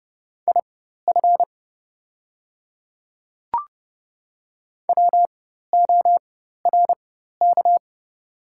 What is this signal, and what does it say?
Morse code, heard as a keyed tone, about 30 wpm, spaced as for 14 wpm, 690 Hz. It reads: IF WORK
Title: 4x Character Spacing - Top 200 Words - 30wpm